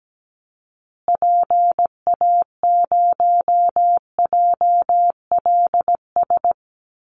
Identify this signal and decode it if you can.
Morse code, heard as a keyed tone, about 17 wpm, 700 Hz. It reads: PA0JLS
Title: Call Signs Repeated 2x - 17wpm